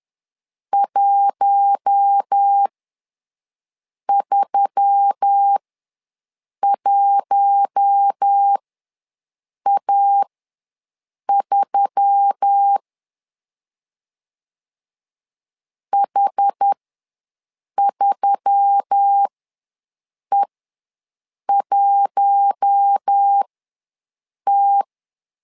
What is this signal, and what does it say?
Morse code, heard as a keyed tone, about 10 wpm, spaced as for 6 wpm, 800 Hz. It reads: 131A3 H3E1T
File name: LEZIONI / 4 Q1S3 ws 10wpm 2437